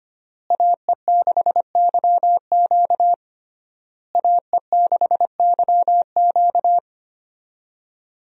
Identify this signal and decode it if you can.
Morse code, heard as a keyed tone, about 25 wpm, 700 Hz. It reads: AE6YQ AE6YQ
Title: USA Call Signs - 5 Characters - Repeated 3x - 25wpm